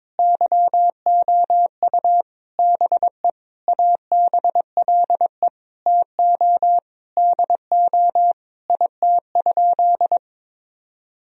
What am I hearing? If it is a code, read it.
Morse code, heard as a keyed tone, about 22 wpm, 700 Hz. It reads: YOU BE ABLE TO DO IT?
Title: Sentences from Top 1000 Words - Review Part 2 - 22wpm